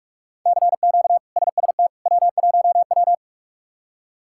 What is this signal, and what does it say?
Morse code, heard as a keyed tone, about 46 wpm, 700 Hz. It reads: CQ SST W1W
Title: K1USN SST QSO Practice - 45wpm